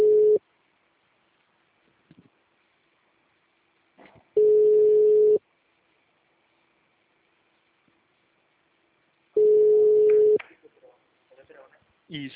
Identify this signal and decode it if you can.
Ringback tone (Europe, 425 Hz), with speech.
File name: out- -2035- - - .30648